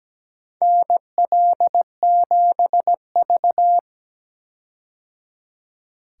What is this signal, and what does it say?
Morse code, heard as a keyed tone, about 17 wpm, 700 Hz. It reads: NL7V
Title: K1USN SST QSO Practice - 17wpm